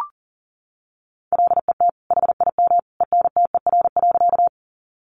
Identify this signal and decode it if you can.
Morse code, heard as a keyed tone, about 40 wpm, 700 Hz. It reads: LET HIM ENTER.
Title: Sentences from Top 1300 Words 40wpm